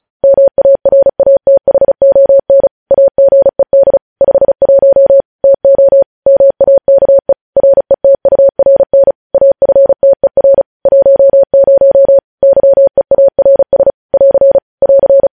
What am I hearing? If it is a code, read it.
Morse code, heard as a keyed tone, about 36 wpm, 550 Hz. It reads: MARATHON AGED 51 TO MAKE RETURN AFTER 10 YEARS <AR> <AR>
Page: News Headlines in Morse Code at 35 WPM